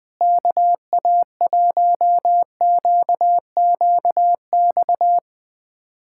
Morse code, heard as a keyed tone, about 20 wpm, 700 Hz. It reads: KA1QQX
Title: USA Callsign DE Callsign - 6 Characters - 20wpm